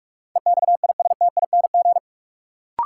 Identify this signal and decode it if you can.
Morse code, heard as a keyed tone, about 42 wpm, 700 Hz. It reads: EXISTING
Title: Sentences from Top 1500 Words - New Word Drill - 45wpm